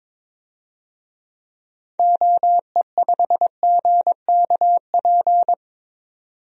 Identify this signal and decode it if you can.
Morse code, heard as a keyed tone, about 22 wpm, 700 Hz. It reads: OE5GKP
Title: DXCC-100 Call Signs - 6 Characters - 22wpm